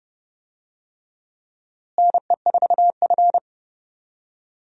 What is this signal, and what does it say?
Morse code, heard as a keyed tone, about 30 wpm, 700 Hz. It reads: NE4F
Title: USA Call Signs - 4 Characters - Repeated 3x - 30wpm